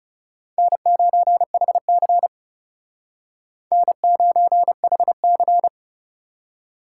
Morse code, heard as a keyed tone, about 36 wpm, 700 Hz. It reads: N9HC N9HC
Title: Speed Racing - USA Call Signs 30wpm